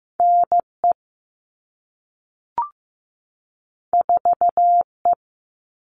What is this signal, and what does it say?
Morse code, heard as a keyed tone, about 15 wpm, 700 Hz. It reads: NE 4E
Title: All Cut Numbers - 2-Digit 15wpm